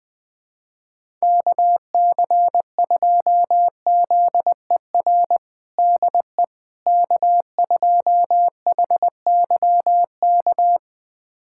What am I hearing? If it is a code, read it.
Morse code, heard as a keyed tone, about 20 wpm, 700 Hz. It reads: KC2ZER DE K2HYK
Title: USA Callsign DE Callsign 20wpm